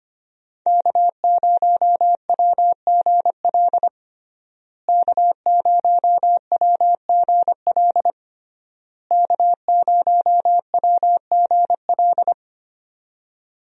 Morse code, heard as a keyed tone, about 25 wpm, 700 Hz. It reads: K0WGL K0WGL K0WGL
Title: USA Call Signs - 5 Characters - Repeated 3x - 25wpm